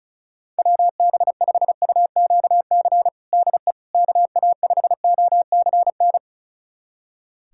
Morse code, heard as a keyed tone, about 36 wpm, 700 Hz. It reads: WB5UQC DE KA5OCN
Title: USA Callsign DE Callsign - 6 Characters - 35wpm